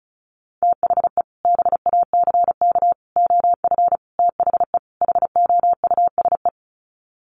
Morse code, heard as a keyed tone, about 34 wpm, 700 Hz. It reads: THE BACK OF THE HOUSE
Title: Sets of 5 Words Encore2 35wpm